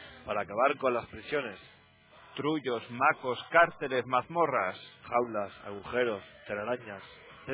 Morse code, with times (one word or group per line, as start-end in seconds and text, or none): none